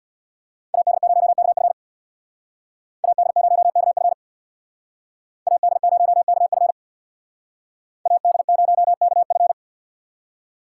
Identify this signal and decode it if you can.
Morse code, heard as a keyed tone, about 63 wpm, 700 Hz. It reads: AD0KF AD0KF AD0KF AD0KF